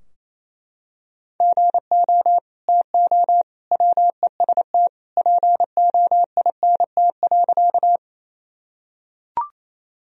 Morse code, heard as a keyed tone, about 27 wpm, 700 Hz. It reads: GO TO WEST POINT.
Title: Sentences from Top 1000 Words - Review Part 2 - 28wpm